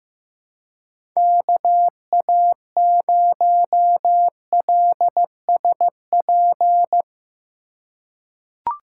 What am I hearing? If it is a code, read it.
Morse code, heard as a keyed tone, about 15 wpm, 700 Hz. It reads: KA0LSP
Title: USA Call Signs - 6 Characters - 2x Character Spacing - 15wpm